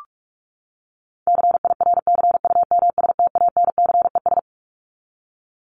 Morse code, heard as a keyed tone, about 44 wpm, 700 Hz. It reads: CIRCUMSTANCES
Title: ES:S Suffix - Top 100 Words with Base Words 45wpm